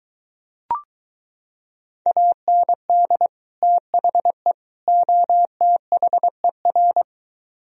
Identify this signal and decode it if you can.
Morse code, heard as a keyed tone, about 23 wpm, 700 Hz. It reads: AND THE OTHER